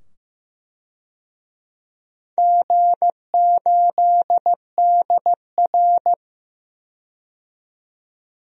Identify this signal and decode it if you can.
Morse code, heard as a keyed tone, about 15 wpm, 700 Hz. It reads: G8DR
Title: DXCC-100 Call Signs - 4 Characters - 15wpm